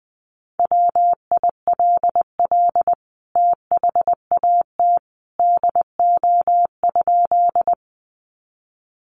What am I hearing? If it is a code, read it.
Morse code, heard as a keyed tone, about 20 wpm, 700 Hz. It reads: WILL THAT DO?
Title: Sentences from Top 100 Words 20wpm